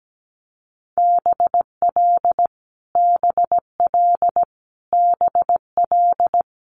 Morse code, heard as a keyed tone, about 17 wpm, 700 Hz. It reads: BL BL BL
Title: Multi-letter Phonemes Repeated 3x 17wpm